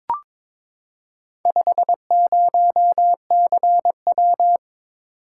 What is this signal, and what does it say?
Morse code, heard as a keyed tone, about 22 wpm, 700 Hz. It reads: 50CW
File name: practice / 22wpm / Four Letter-Number Combinations - 22wpm